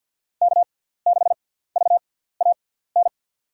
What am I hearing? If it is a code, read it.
Morse code, heard as a keyed tone, about 50 wpm, spaced as for 17 wpm, 700 Hz. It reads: K6VAN
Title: USA Call Signs - 5 Characters - 6x Character Spacing - 50wpm